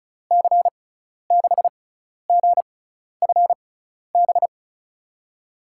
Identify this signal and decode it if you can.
Morse code, heard as a keyed tone, about 36 wpm, spaced as for 12 wpm, 700 Hz. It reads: C6GFB
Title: USA Call Signs - 6 Characters - 6x Character Spacing - 35wpm